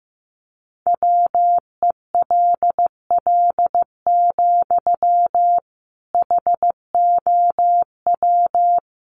Morse code, heard as a keyed tone, about 15 wpm, 700 Hz. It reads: WELL, HOW
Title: Sentences from Top 200 Words 15wpm